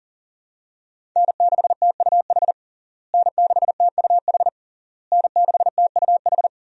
Morse code, heard as a keyed tone, about 40 wpm, 700 Hz. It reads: N6TUH N6TUH N6TUH
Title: USA Call Signs - 5 Characters - Repeated 3x - 40wpm